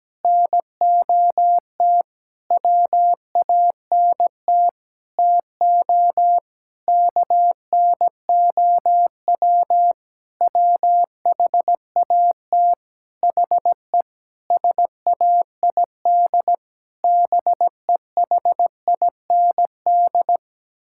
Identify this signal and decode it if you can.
Morse code, heard as a keyed tone, about 17 wpm, 700 Hz. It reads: NOT WANT TO KNOW WHAT HE SAID BEHIND
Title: Sentences from Top 1000 Words - Review Part 6 - 17wpm